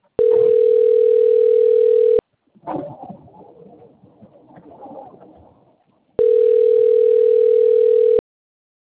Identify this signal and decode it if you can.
Ringback tone (North America), 440 + 480 Hz.
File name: rg-1600-unknown- - - .23496